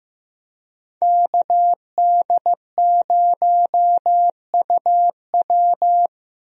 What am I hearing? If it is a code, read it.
Morse code, heard as a keyed tone, about 15 wpm, 700 Hz. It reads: KD0UW
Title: USA Call Signs - 5 Characters - Repeated 2x - 15wpm